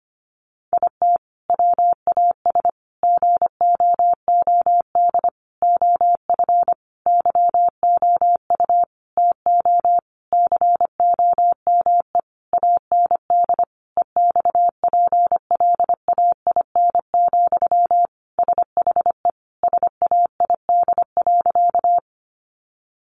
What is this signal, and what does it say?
Morse code, heard as a keyed tone, about 25 wpm, 700 Hz. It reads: IT WAS GOOD OF YOU TO COME AND EXPLAIN, SHE SAID.
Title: Sentences from Top 1100 Words 25wpm